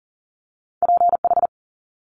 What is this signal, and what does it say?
Morse code, heard as a keyed tone, about 40 wpm, 700 Hz. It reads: PH
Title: Multi-letter Phonemes Quick Fire 40wpm